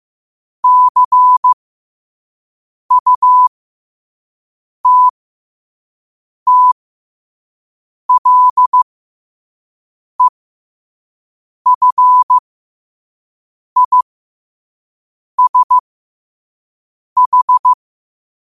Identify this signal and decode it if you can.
Morse code, heard as a keyed tone, about 15 wpm, spaced as for 5 wpm, 1000 Hz. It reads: CUTTLEFISH
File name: qotdfiles / 5wpm